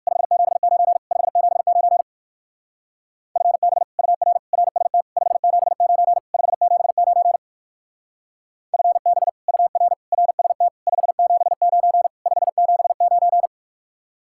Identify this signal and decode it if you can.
Morse code, heard as a keyed tone, about 58 wpm, 700 Hz. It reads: 579 579 FB UR RST 579 579 FB UR RST 579 579